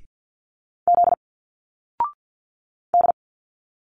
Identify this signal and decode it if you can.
Morse code, heard as a keyed tone, about 50 wpm, 700 Hz. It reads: Z D
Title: Single Letters 50wpm